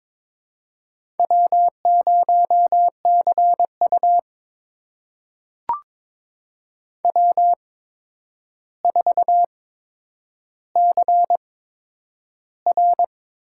Morse code, heard as a keyed tone, about 22 wpm, spaced as for 9 wpm, 700 Hz. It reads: W0CU W4CR
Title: USA Call Signs - 4 Characters - 8x Character Spacing - 22wpm